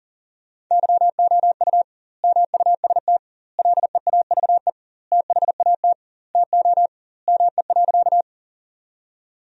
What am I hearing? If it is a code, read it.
Morse code, heard as a keyed tone, about 40 wpm, 700 Hz. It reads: YOU MUST LEAVE THAT TO ME.
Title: 2x Word-Spacing - Sentences from Top 400 Words - 40wpm